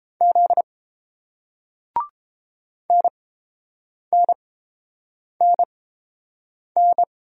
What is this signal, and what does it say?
Morse code, heard as a keyed tone, about 30 wpm, spaced as for 7 wpm, 690 Hz. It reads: Z NNNN